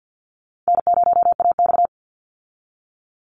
Morse code, heard as a keyed tone, about 50 wpm, 700 Hz. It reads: N0AX
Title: Easy Call Signs - 50wpm